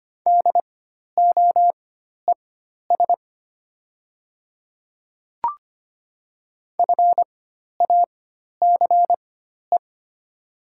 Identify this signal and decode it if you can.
Morse code, heard as a keyed tone, about 25 wpm, spaced as for 12 wpm, 700 Hz. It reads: DOES FACE
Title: 4x Character Spacing - Top 300 Words - 25wpm